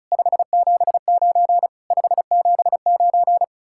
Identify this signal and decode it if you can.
Morse code, heard as a keyed tone, about 43 wpm, 700 Hz. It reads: 579 579
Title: Speed-Racing Common QSO Phrases - 35wpm